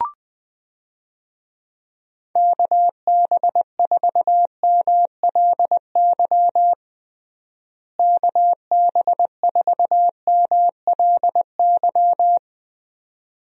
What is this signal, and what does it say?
Morse code, heard as a keyed tone, about 20 wpm, 700 Hz. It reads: KB4MLY KB4MLY
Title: USA Call Signs - 6 Characters - Repeated 2x - 20wpm